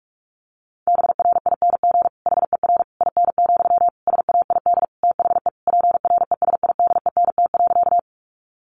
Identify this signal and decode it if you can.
Morse code, heard as a keyed tone, about 44 wpm, 700 Hz. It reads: BRING HER IN, SAID THE PRESIDENT.
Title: Sentences from Top 1000 Words 45wpm